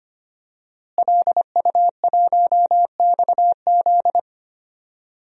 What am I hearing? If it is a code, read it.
Morse code, heard as a keyed tone, about 25 wpm, 700 Hz. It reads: LU1XZ
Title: DXCC-100 Call Signs - 5 Characters - Repeated 2x - 25wpm